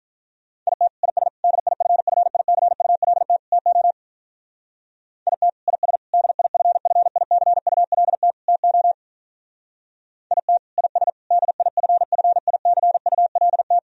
Morse code, heard as a keyed tone, about 44 wpm, 700 Hz. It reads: IT IS DIFFICULT TO IT IS DIFFICULT TO IT IS DIFFICULT